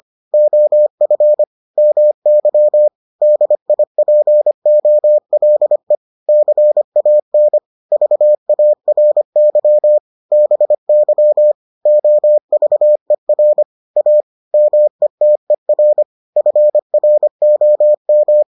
Morse code, heard as a keyed tone, about 25 wpm, 600 Hz. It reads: OF MY DIPOLE CAN VARY BY OVER A METER FROM